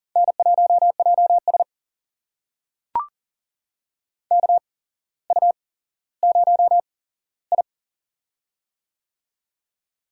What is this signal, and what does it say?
Morse code, heard as a keyed tone, about 42 wpm, spaced as for 17 wpm, 700 Hz. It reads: N1JS KU0I
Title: USA Call Signs - 4 Characters - 8x Character Spacing - 40wpm